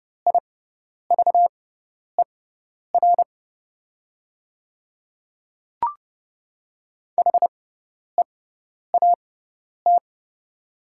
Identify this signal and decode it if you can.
Morse code, heard as a keyed tone, about 30 wpm, spaced as for 10 wpm, 700 Hz. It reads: IVER HEAT
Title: 6x Character Spacing - Top 500 Words - 30wpm